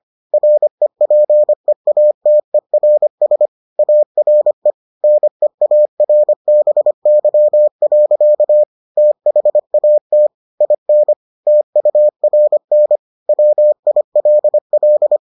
Morse code, heard as a keyed tone, about 25 wpm, 600 Hz. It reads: REPEATERS ARE NEARBY. THAT IN TURN WILL